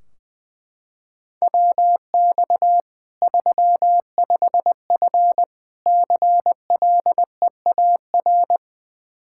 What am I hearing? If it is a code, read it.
Morse code, heard as a keyed tone, about 21 wpm, 700 Hz. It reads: WX 35F CLEAR
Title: K1USN SST QSO Practice 20wpm